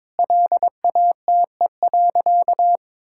Morse code, heard as a keyed tone, about 22 wpm, 700 Hz. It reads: LATE.
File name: practice / 22wpm / Sentences with 3-Word Phrases - 22wpm